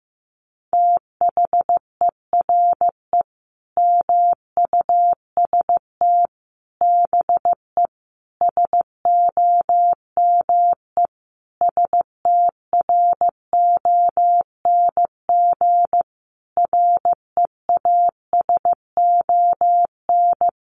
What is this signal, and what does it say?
Morse code, heard as a keyed tone, about 15 wpm, 700 Hz. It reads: THERE MUST BE SOME STRONG REASON
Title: Sentences from Top 700 Words 15wpm